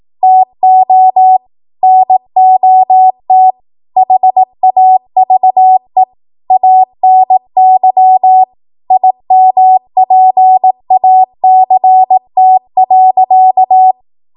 Morse code, heard as a keyed tone, about 18 wpm, 750 Hz. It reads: TO NOT HAVE ANY IMPACT.
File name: Morse / 18WPM